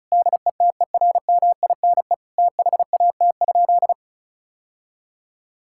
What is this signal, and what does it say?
Morse code, heard as a keyed tone, about 36 wpm, 700 Hz. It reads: DETERMINE THAT?
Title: Sentences from Top 1000 Words - Review Part 5 - 35wpm